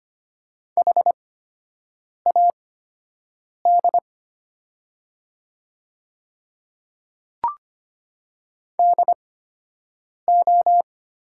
Morse code, heard as a keyed tone, about 25 wpm, spaced as for 7 wpm, 700 Hz. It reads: HAD DO
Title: 8x Character Spacing - Top 100 Words - 25wpm